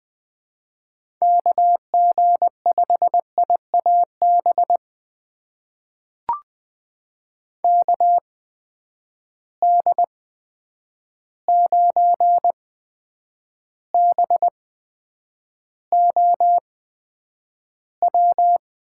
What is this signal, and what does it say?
Morse code, heard as a keyed tone, about 20 wpm, spaced as for 9 wpm, 700 Hz. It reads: KG5IAB KD9BOW
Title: USA Call Signs - 6 Characters - 8x Character Spacing - 20wpm